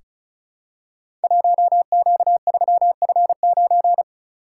Morse code, heard as a keyed tone, about 36 wpm, 700 Hz. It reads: 1Q3F9
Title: Code Groups - 5 Letter:Numbers - 35wpm